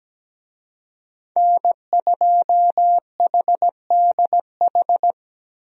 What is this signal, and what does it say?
Morse code, heard as a keyed tone, about 17 wpm, 700 Hz. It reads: N2HDH
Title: USA Call Signs - 5 Characters - Repeated 2x - 17wpm